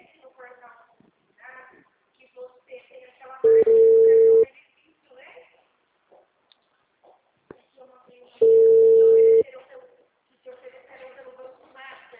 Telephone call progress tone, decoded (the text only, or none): ringback tone